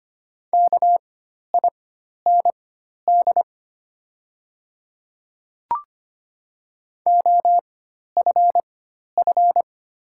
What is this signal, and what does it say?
Morse code, heard as a keyed tone, about 25 wpm, spaced as for 12 wpm, 700 Hz. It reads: KIND OFF
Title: 4x Character Spacing - Top 300 Words - 25wpm